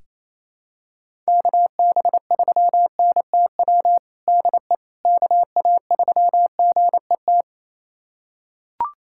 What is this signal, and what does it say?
Morse code, heard as a keyed tone, about 28 wpm, 700 Hz. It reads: KB3NTW DE KA3GET
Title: USA Callsign DE Callsign - 6 Characters - 28wpm